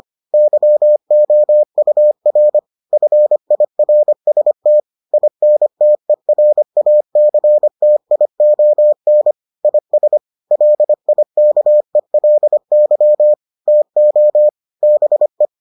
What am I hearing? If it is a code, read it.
Morse code, heard as a keyed tone, about 25 wpm, 600 Hz. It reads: YOUR FIRST INTERACTION IS LIKELY TO BE